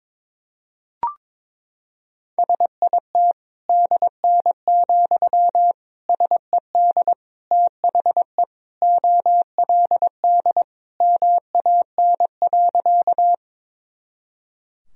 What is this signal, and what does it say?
Morse code, heard as a keyed tone, about 22 wpm, 700 Hz. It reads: SIT DN, SED THE OLD MAN.
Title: Sentences from Top 500 Words and Abbreviations 22wpm